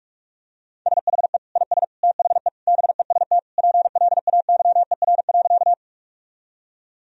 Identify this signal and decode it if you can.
Morse code, heard as a keyed tone, about 46 wpm, 700 Hz. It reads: SHE IS THE BEST PLAYER.
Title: Sentences from Top 1000 Words - Review Part 4 - 45wpm